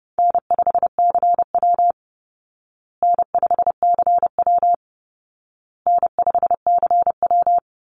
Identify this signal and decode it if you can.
Morse code, heard as a keyed tone, about 30 wpm, 700 Hz. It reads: N5CW N5CW N5CW
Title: Easy Call Signs Repeated 3x - 30wpm